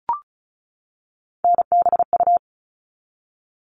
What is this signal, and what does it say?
Morse code, heard as a keyed tone, about 34 wpm, 700 Hz. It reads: NBU
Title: All Cut Numbers - 3-Digit 35wpm